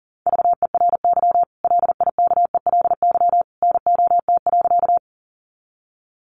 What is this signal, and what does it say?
Morse code, heard as a keyed tone, about 39 wpm, 700 Hz. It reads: VERY LIKELY NOT.
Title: Sentences from Top 600 Words 40wpm